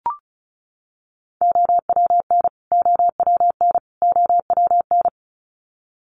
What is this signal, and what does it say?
Morse code, heard as a keyed tone, about 34 wpm, 700 Hz. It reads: OWN OWN OWN
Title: Top 300 Words Repeated 35wpm